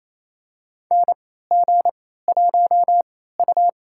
Morse code, heard as a keyed tone, about 28 wpm, spaced as for 16 wpm, 700 Hz. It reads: NG1U
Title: USA Call Signs - 4 Characters - 3x Character Spacing - 28wpm